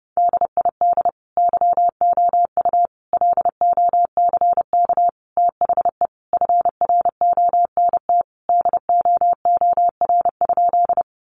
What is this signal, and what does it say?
Morse code, heard as a keyed tone, about 30 wpm, 700 Hz. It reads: DID YOU LOCK THE FRONT DOOR?
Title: Sentences from Top 1300 Words 30wpm